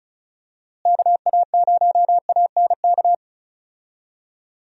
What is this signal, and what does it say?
Morse code, heard as a keyed tone, about 34 wpm, 700 Hz. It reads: KA0ANK
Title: USA Call Signs - 6 Characters - Repeated 3x - 35wpm